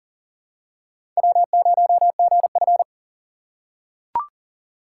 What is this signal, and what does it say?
Morse code, heard as a keyed tone, about 40 wpm, 700 Hz. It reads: W0GF
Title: USA Call Signs - 4 Characters - 3x Character Spacing - 40wpm